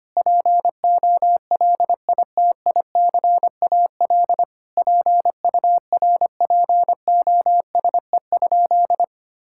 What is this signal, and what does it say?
Morse code, heard as a keyed tone, about 25 wpm, 700 Hz. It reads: POLITICAL PURPOSE?
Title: Sentences from Top 1000 Words - Review Part 5 - 25wpm